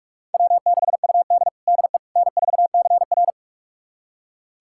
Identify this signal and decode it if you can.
Morse code, heard as a keyed tone, about 42 wpm, 700 Hz. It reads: W6UD DE N4CR
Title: USA Callsign DE Callsign - 4 Characters - 45wpm